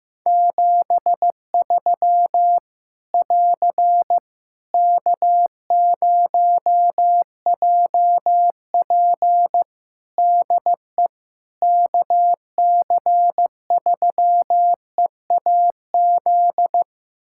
Morse code, heard as a keyed tone, about 15 wpm, 700 Hz. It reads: 73 <AR> K0JP DE KC3EAZ
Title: Practice QSO - Exchange 4 - Ending - 15wpm